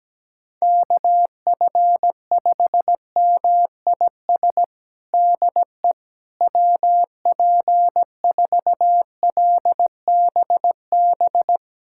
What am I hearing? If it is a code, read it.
Morse code, heard as a keyed tone, about 17 wpm, 700 Hz. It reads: KF5MIS DE WP4LBB
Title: USA Callsign DE Callsign - 6 Characters - 17wpm